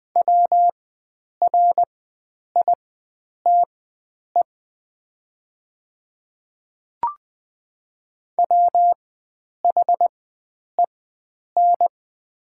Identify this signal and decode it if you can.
Morse code, heard as a keyed tone, about 20 wpm, spaced as for 9 wpm, 700 Hz. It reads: WRITE WHEN
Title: 4x Character Spacing - Top 100 Words - 20wpm